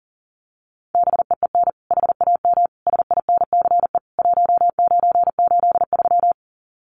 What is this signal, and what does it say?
Morse code, heard as a keyed tone, about 40 wpm, 700 Hz. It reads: BEEN HAM SINCE 1983
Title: Common QSO Phrases 40wpm